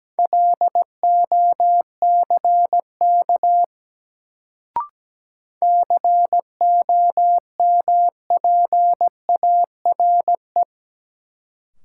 Morse code, heard as a keyed tone, about 17 wpm, 700 Hz. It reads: LOCK COMPARE
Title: Sentences from Top 1300 Words - New Word Drill Alt - 17wpm